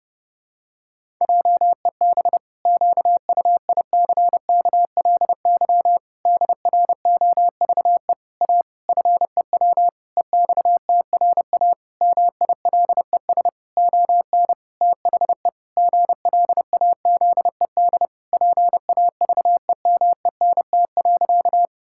Morse code, heard as a keyed tone, about 30 wpm, 700 Hz. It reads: JEB QUICKLY DROVE A FEW EXTRA MILES ON THE GLAZED PAVEMENT.
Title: Pangram Challenge v2 - 30wpm